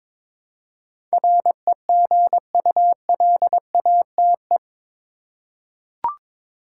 Morse code, heard as a keyed tone, about 21 wpm, 700 Hz. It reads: REGULATE
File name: practice / 22wpm / Eight Letter Words - 22wpm